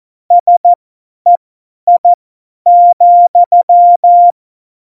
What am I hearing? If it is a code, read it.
Morse code, heard as a keyed tone, about 14 wpm, spaced as for 10 wpm, 700 Hz. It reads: SEI,
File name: Morsekurs / Witze / Witz 24. November 2020, 50 F70